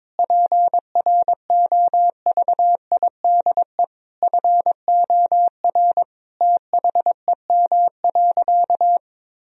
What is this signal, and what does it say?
Morse code, heard as a keyed tone, about 22 wpm, 700 Hz. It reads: PROVIDE FOR THEM.
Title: Sentences from Top 1000 Words - Review Part 3 - 22wpm